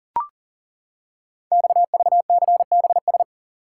Morse code, heard as a keyed tone, about 40 wpm, 690 Hz. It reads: XVCBS
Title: Code Groups - 5 Letters - 40wpm